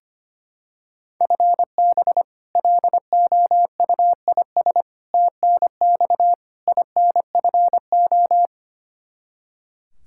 Morse code, heard as a keyed tone, about 25 wpm, 700 Hz. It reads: FB LOUIS TNX INFO
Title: Practice QSO - Exchange 3 - More Info - 25wpm